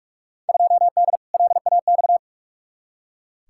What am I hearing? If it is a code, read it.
Morse code, heard as a keyed tone, about 46 wpm, 700 Hz. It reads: 2D LAX
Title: Field Day Exchanges - 45wpm